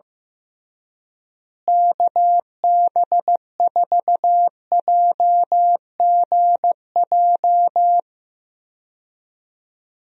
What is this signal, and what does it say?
Morse code, heard as a keyed tone, about 15 wpm, 700 Hz. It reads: KB4JGJ
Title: USA Call Signs - 6 Characters - Repeated 3x - 15wpm